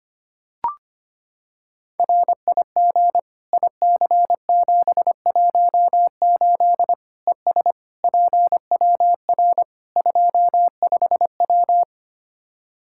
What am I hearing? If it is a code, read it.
Morse code, heard as a keyed tone, about 25 wpm, 700 Hz. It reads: RIG IC718 ES PWR 25W
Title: Practice QSO - Exchange 2 - Info - 25wpm